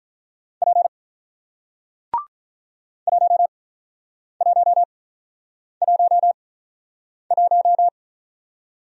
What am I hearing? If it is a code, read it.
Morse code, heard as a keyed tone, about 46 wpm, spaced as for 9 wpm, 700 Hz. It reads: P 1111